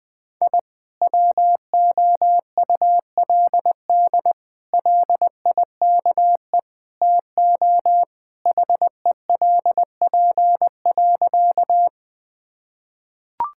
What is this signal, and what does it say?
Morse code, heard as a keyed tone, about 21 wpm, 700 Hz. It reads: I WOULD LIKE TO HELP.
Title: Sentences with 4-Word Phrases - Spoken Beginning - 20wpm